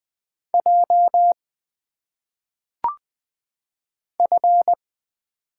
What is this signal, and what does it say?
Morse code, heard as a keyed tone, about 20 wpm, 700 Hz. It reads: J F
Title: Single Letters - Phonetic Pronunciation 20wpm